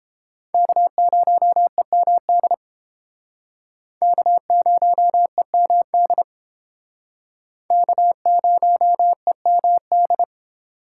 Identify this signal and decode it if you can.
Morse code, heard as a keyed tone, about 29 wpm, 700 Hz. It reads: K0EMD K0EMD K0EMD